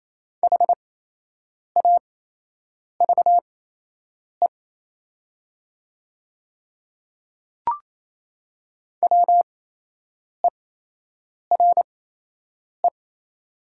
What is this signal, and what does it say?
Morse code, heard as a keyed tone, about 28 wpm, spaced as for 8 wpm, 700 Hz. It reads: HAVE WERE
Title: 8x Character Spacing - Top 200 Words - 28wpm